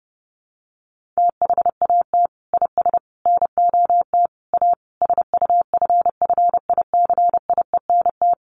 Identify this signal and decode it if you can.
Morse code, heard as a keyed tone, about 30 wpm, 700 Hz. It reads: THAT IS NOT A SUFFICIENT
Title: Sentences from Top 1400 Words 30wpm